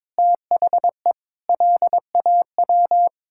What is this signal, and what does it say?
Morse code, heard as a keyed tone, about 22 wpm, 700 Hz. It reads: THE LAW
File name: practice / 22wpm / Sets of 7 Words - 22wpm